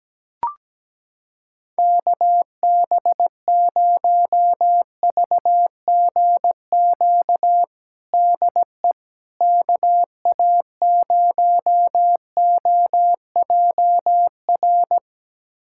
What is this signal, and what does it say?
Morse code, heard as a keyed tone, about 17 wpm, 700 Hz. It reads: KB0VGQ DE KA0OJR
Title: USA Callsign DE Callsign 17wpm